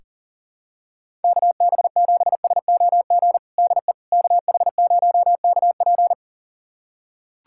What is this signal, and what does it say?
Morse code, heard as a keyed tone, about 40 wpm, 700 Hz. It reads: KB7SOG DE KH0KP
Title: USA Callsign DE Callsign 40wpm